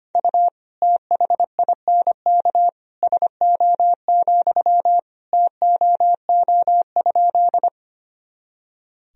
Morse code, heard as a keyed tone, about 25 wpm, 700 Hz. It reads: U THINK SO, TOO?
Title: Sentences from Top 200 Words and Abbreviations 25wpm